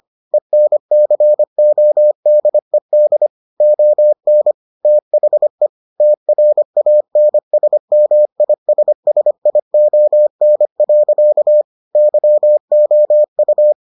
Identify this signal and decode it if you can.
Morse code, heard as a keyed tone, about 25 wpm, 600 Hz. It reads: ENCODED ON THE TRANSMISSION. YOU